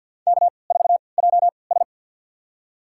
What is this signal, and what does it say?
Morse code, heard as a keyed tone, about 51 wpm, spaced as for 28 wpm, 700 Hz. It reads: K4JS
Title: USA Call Signs - 4 Characters - 3x Character Spacing - 50wpm